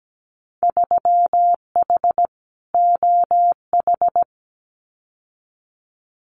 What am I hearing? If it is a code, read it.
Morse code, heard as a keyed tone, about 17 wpm, 700 Hz. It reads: 3H OH
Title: Winter Field Day Exchanges Encore 17wpm